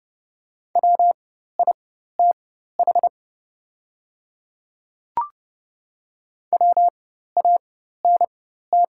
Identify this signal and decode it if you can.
Morse code, heard as a keyed tone, about 30 wpm, spaced as for 14 wpm, 700 Hz. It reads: WITH WANT
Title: 4x Character Spacing - Top 200 Words - 30wpm